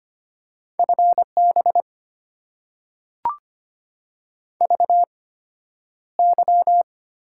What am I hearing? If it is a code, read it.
Morse code, heard as a keyed tone, about 25 wpm, spaced as for 11 wpm, 690 Hz. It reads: FB VY
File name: Top 90 QSO Elements - 8x Character Spacing - 25wpm